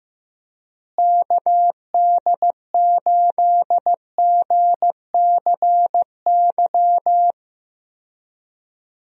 Morse code, heard as a keyed tone, about 15 wpm, 700 Hz. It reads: KD8GCY
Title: USA Call Signs - 6 Characters - Repeated 3x - 15wpm